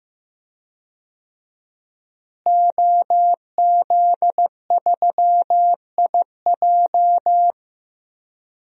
Morse code, heard as a keyed tone, about 16 wpm, 700 Hz. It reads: OZ3IJ
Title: DXCC-100 Call Signs - 5 Characters - Repeated 3x - 15wpm